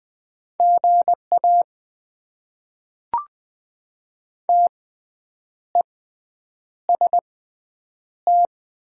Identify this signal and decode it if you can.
Morse code, heard as a keyed tone, about 20 wpm, spaced as for 8 wpm, 700 Hz. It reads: GA TEST
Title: Top 90 QSO Elements - 6x Character Spacing - 20wpm